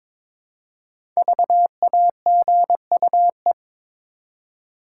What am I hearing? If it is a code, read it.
Morse code, heard as a keyed tone, about 22 wpm, 700 Hz. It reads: VAGUE
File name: 5 Letter Words - First 3 Letters Given - 22wpm